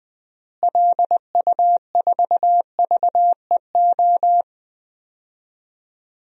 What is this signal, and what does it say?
Morse code, heard as a keyed tone, about 21 wpm, 700 Hz. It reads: LU4VEO
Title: DXCC-100 Call Signs - 6 Characters - Repeated 2x - 20wpm